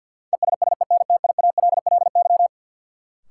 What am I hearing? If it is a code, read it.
Morse code, heard as a keyed tone, about 52 wpm, 700 Hz. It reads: ESSENTIALLY
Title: Sentences from Top 1300 Words - New Word Drill - 50wpm